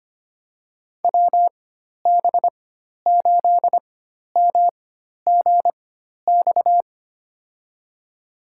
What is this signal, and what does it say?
Morse code, heard as a keyed tone, about 25 wpm, spaced as for 12 wpm, 700 Hz. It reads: WB8MGX
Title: USA Call Signs - 4x Character Spacing - 25wpm